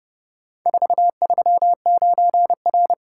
Morse code, heard as a keyed tone, about 29 wpm, 700 Hz. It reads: 439R